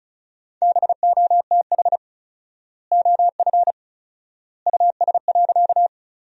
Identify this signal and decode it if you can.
Morse code, heard as a keyed tone, about 34 wpm, 700 Hz. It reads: BOTH OF US.
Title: 4x Word-Spacing - Sentences from Top 300 Words - 35wpm